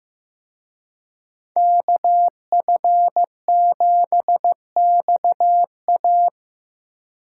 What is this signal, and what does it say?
Morse code, heard as a keyed tone, about 18 wpm, 700 Hz. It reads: KF7XA
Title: Speed Racing - USA 2x2 Call Signs 15wpm